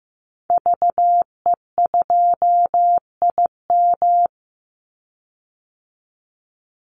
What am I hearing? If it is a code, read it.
Morse code, heard as a keyed tone, about 15 wpm, 700 Hz. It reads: VE2IM
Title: Call Signs Repeated 2x - 15wpm